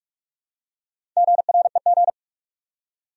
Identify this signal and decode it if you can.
Morse code, heard as a keyed tone, about 46 wpm, 700 Hz. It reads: GREG
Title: USA Common Names - Max 4 Letters - 45wpm